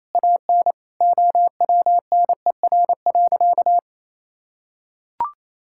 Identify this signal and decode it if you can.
Morse code, heard as a keyed tone, about 28 wpm, 700 Hz. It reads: AN OWNER.
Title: Sentences from Top 1600 Words 28wpm